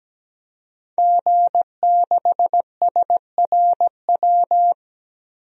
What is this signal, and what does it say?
Morse code, heard as a keyed tone, about 17 wpm, 700 Hz. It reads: G6SRW
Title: DXCC-100 Call Signs - 5 Characters - Repeated 2x - 17wpm